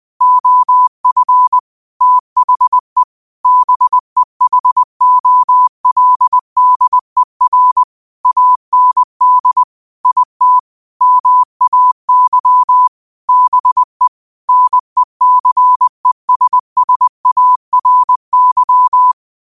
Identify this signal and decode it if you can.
Morse code, heard as a keyed tone, about 20 wpm, 1000 Hz. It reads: OF THE BEHOLDER AND IT MAY BE NECESSARY
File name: qotdfiles / 20wpm